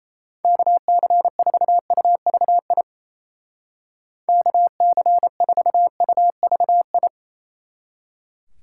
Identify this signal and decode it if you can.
Morse code, heard as a keyed tone, about 30 wpm, 700 Hz. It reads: KC4UVI KC4UVI
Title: Speed Racing - USA Call Signs 28wpm